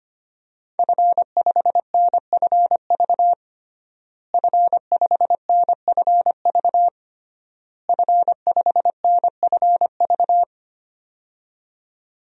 Morse code, heard as a keyed tone, about 25 wpm, 700 Hz. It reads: F5NFV F5NFV F5NFV
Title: DXCC-100 Call Signs - 5 Characters - Repeated 3x - 25wpm